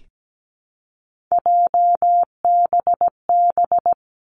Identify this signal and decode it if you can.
Morse code, heard as a keyed tone, about 17 wpm, 700 Hz. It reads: JBB
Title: Three Challenging Letter-Number Combinations 17wpm